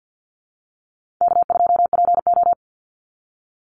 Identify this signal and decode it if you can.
Morse code, heard as a keyed tone, about 50 wpm, 700 Hz. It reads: K2PO
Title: Call Signs - 50wpm